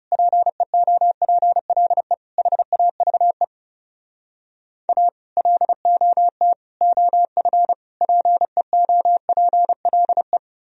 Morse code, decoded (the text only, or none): PEOPLE HAVE A LOT OF PEOPLE